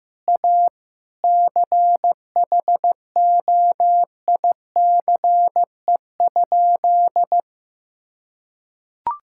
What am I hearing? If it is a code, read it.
Morse code, heard as a keyed tone, about 15 wpm, 700 Hz. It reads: A CHOICE?
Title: Sentences from Top 1000 Words - Review Part 6 - 15wpm